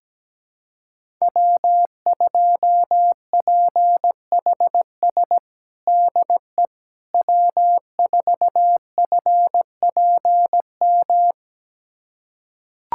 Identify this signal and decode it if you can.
Morse code, heard as a keyed tone, about 17 wpm, 700 Hz. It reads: W2PHS DE W4FPM
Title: USA Callsign DE Callsign - 5 Characters - 17wpm